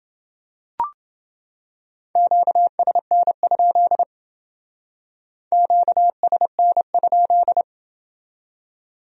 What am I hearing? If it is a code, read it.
Morse code, heard as a keyed tone, about 28 wpm, 700 Hz. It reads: QSN? QSN?